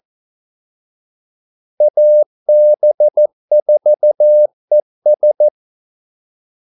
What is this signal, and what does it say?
Morse code, heard as a keyed tone, about 14 wpm, 600 Hz. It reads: AB4ES
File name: Callsigns-14wpm